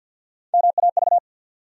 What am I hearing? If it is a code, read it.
Morse code, heard as a keyed tone, about 53 wpm, 700 Hz. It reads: MAV
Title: USA Call Sign Postfixes - 3 Letters 50wpm